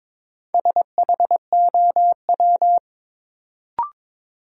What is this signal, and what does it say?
Morse code, heard as a keyed tone, about 22 wpm, 700 Hz. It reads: SHOW
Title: Sentences from Top 200 Words - New Word Drill - 22wpm